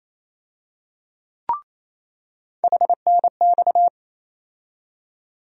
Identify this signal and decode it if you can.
Morse code, heard as a keyed tone, about 28 wpm, 700 Hz. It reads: HNX
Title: USA Call Sign Postfixes - 3 Letters 28wpm